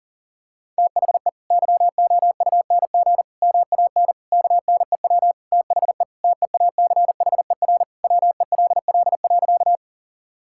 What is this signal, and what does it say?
Morse code, heard as a keyed tone, about 40 wpm, 700 Hz. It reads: THE YOUNG MAN KNEW THE TEACHER WELL.
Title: Sentences from Top 1000 Words - Review Part 6 - 40wpm